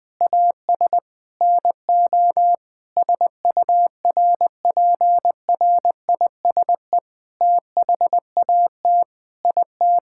Morse code, heard as a keyed tone, about 20 wpm, 700 Hz. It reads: AS NO SURPRISE THAT IT
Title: Sentences with 8-Word Phrases - 20wpm